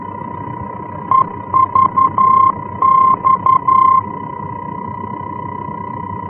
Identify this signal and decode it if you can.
Morse code, heard as a keyed tone, about 11 wpm, 1050 Hz. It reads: EVX
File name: france / mx / EVX-F-VOR